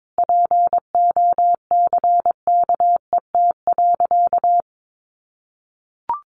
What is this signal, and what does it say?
Morse code, heard as a keyed tone, about 22 wpm, 700 Hz. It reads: POCKET.
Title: A Sound Of Thunder 22wpm